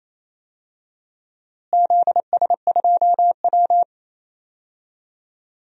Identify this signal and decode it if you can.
Morse code, heard as a keyed tone, about 27 wpm, 700 Hz. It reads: ZS2W
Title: DXCC-100 Call Signs - 4 Characters - Repeated 3x - 28wpm